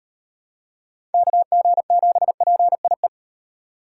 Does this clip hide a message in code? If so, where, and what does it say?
Yes: Morse code from 1.1 to 3.1 s, heard as a keyed tone, about 36 wpm, 700 Hz. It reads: KG7PIE